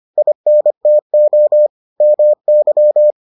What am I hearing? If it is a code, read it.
Morse code, heard as a keyed tone, about 25 wpm, 600 Hz. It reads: INTO MY